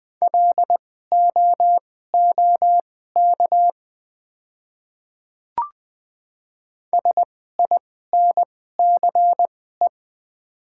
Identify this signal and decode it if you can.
Morse code, heard as a keyed tone, about 20 wpm, spaced as for 15 wpm, 700 Hz. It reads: LOOK SINCE
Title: 2x Character Spacing - Top 500 Words - 20wpm